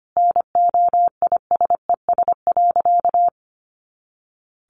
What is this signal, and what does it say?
Morse code, heard as a keyed tone, about 25 wpm, 700 Hz. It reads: NOISES.
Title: Easy Phrases 25wpm